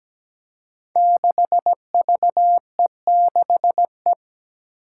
Morse code, heard as a keyed tone, about 17 wpm, 700 Hz. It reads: 6VE6E
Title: Five Challenging Letter-Number Combinations 17wpm